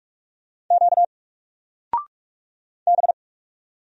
Morse code, heard as a keyed tone, about 44 wpm, 700 Hz. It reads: Q B
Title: Single Character (Letter:Number:Punctuation:Prosign) Alt 45wpm